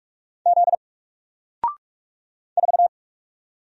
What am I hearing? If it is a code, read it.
Morse code, heard as a keyed tone, about 44 wpm, 700 Hz. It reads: Z 4
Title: Single Letter-Number 45wpm